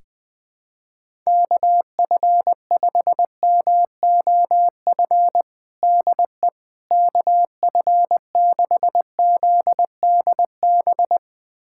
Morse code, heard as a keyed tone, about 20 wpm, 700 Hz. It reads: KF5MOF DE KF6ZDB
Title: USA Callsign DE Callsign - 6 Characters - 20wpm